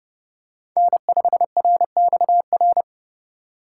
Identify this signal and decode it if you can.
Morse code, heard as a keyed tone, about 30 wpm, 700 Hz. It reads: N5RXR
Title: USA Call Signs - 4x Character Spacing - 30wpm